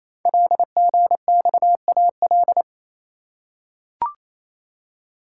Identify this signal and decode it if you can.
Morse code, heard as a keyed tone, about 27 wpm, 700 Hz. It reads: LGXAL
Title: Code Groups - 5 Letters - 28wpm